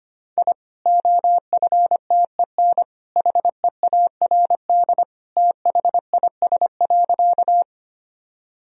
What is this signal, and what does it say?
Morse code, heard as a keyed tone, about 25 wpm, 700 Hz. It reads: I OFTEN HEARD THIS.
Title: Sentences from Top 1000 Words - Review Part 1 - 25wpm